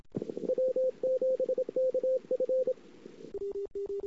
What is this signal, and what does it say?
Morse code, heard as a keyed tone, about 27 wpm, 520 Hz. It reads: W7KF
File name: CW / 2008-ss-cw-k5tr-op-n5rz -21:00